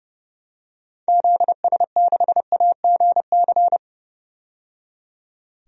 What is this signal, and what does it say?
Morse code, heard as a keyed tone, about 30 wpm, 700 Hz. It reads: ZS6AGC
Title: DXCC-100 Call Signs - 6 Characters - 30wpm